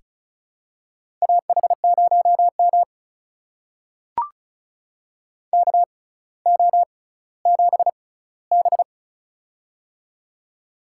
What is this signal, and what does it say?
Morse code, heard as a keyed tone, about 36 wpm, spaced as for 18 wpm, 700 Hz. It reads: AH0M KO7B
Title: USA Call Signs - 4 Characters - 6x Character Spacing - 35wpm